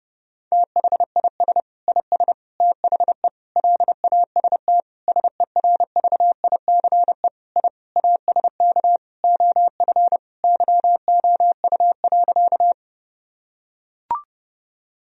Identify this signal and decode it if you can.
Morse code, heard as a keyed tone, about 30 wpm, 700 Hz. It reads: THIS IS THE LAST SERVICE I ASK OF YOU.
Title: Sentences from Top 1000 Words - Review Part 4 - 30wpm